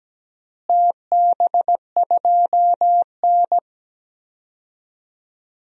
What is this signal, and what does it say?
Morse code, heard as a keyed tone, about 17 wpm, 700 Hz. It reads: TB2N
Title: DXCC-100 Call Signs - 4 Characters - Repeated 3x - 17wpm